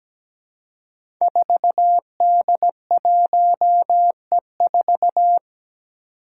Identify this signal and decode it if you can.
Morse code, heard as a keyed tone, about 17 wpm, 700 Hz. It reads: 4D1E4
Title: Five Challenging Letter-Number Combinations 17wpm